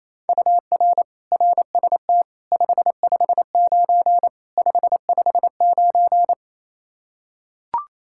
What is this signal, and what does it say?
Morse code, heard as a keyed tone, about 28 wpm, 700 Hz. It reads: UR RST 559 559
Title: Practice QSO - Exchange 1 - Beginning - 28wpm